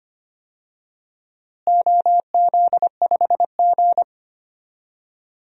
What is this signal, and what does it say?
Morse code, heard as a keyed tone, about 25 wpm, 700 Hz. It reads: OZ5G
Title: DXCC-100 Call Signs - 4 Characters - Repeated 3x - 25wpm